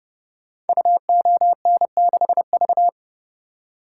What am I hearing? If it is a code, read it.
Morse code, heard as a keyed tone, about 30 wpm, 700 Hz. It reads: UON6V
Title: Code Groups - 5 Letter:Numbers - 30wpm